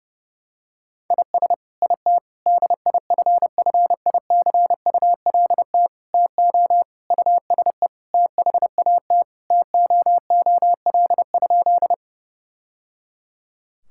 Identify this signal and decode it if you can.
Morse code, heard as a keyed tone, about 30 wpm, 700 Hz. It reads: IS IT DIFFICULT TO USE THAT TOOL?
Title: Sentences from Top 1000 Words - Review Part 6 - 30wpm